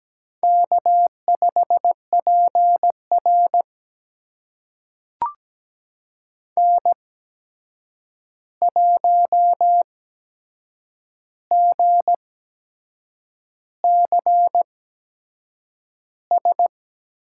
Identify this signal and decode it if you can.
Morse code, heard as a keyed tone, about 17 wpm, spaced as for 7 wpm, 700 Hz. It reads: K5PR N1GCS
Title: USA Call Signs - 8x Character Spacing - 17wpm